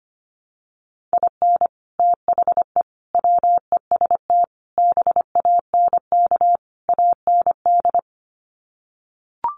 Morse code, heard as a keyed tone, about 25 wpm, 700 Hz. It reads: IN THE WEST BANK AND
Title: Sets of 5 Words Encore2 25wpm